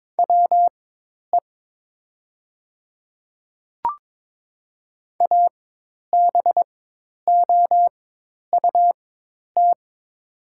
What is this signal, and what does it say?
Morse code, heard as a keyed tone, about 22 wpm, spaced as for 10 wpm, 700 Hz. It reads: WE ABOUT
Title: 4x Character Spacing - Top 100 Words - 22wpm